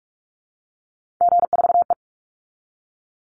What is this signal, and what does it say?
Morse code, heard as a keyed tone, about 44 wpm, 700 Hz. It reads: G4E
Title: Call Signs Repeated 2x - 45wpm